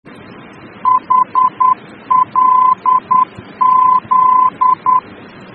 Morse code, heard as a keyed tone, about 10 wpm, 1020 Hz. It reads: HLZ
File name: germany / mx / HLZ-D-VOR